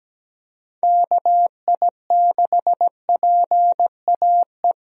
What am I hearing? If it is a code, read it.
Morse code, heard as a keyed tone, about 17 wpm, 700 Hz. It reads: KI6PAE
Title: USA Call Signs - 6 Characters - 17wpm